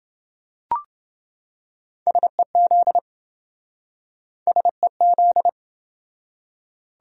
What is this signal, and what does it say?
Morse code, heard as a keyed tone, about 26 wpm, 700 Hz. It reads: SEZ SEZ